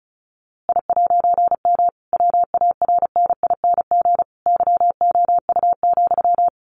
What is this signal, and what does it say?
Morse code, heard as a keyed tone, about 34 wpm, 700 Hz. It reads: I'M WARNING YOU,
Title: A Sound Of Thunder 35wpm